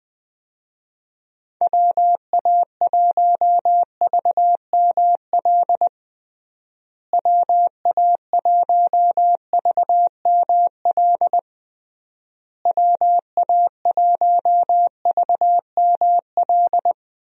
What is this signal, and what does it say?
Morse code, heard as a keyed tone, about 20 wpm, 700 Hz. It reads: WA1VML WA1VML WA1VML
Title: USA Call Signs - 6 Characters - Repeated 3x - 20wpm